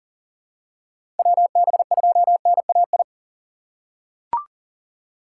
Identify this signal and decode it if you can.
Morse code, heard as a keyed tone, about 40 wpm, 700 Hz. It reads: WB2NAI
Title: USA Call Signs - 6 Characters - 4x Character Spacing - 40wpm